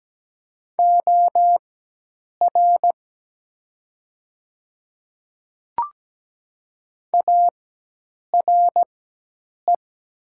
Morse code, heard as a keyed tone, about 17 wpm, spaced as for 8 wpm, 700 Hz. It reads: OR ARE
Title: 4x Character Spacing - Top 100 Words - 17wpm